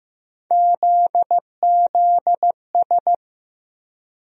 Morse code, heard as a keyed tone, about 15 wpm, 700 Hz. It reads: ZZS
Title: USA Call Sign Postfixes - 3 Letters 15wpm